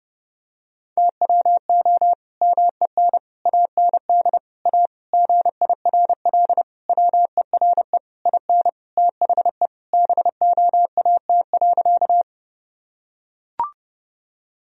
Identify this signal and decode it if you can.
Morse code, heard as a keyed tone, about 30 wpm, 700 Hz. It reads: TWO MEN AND A GIRL WERE IN THE BOAT.
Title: Sentences from Top 1000 Words - Review Part 3 - 30wpm